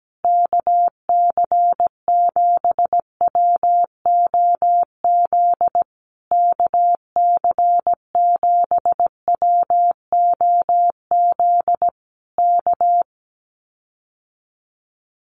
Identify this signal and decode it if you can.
Morse code, heard as a keyed tone, about 17 wpm, 700 Hz. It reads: KC7WOZ KC7WOZ K
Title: Common QSO Phrases Encore 17wpm